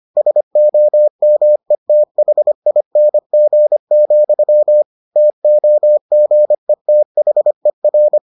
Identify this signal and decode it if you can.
Morse code, heard as a keyed tone, about 25 wpm, 600 Hz. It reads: SOMETHING, TOGETHER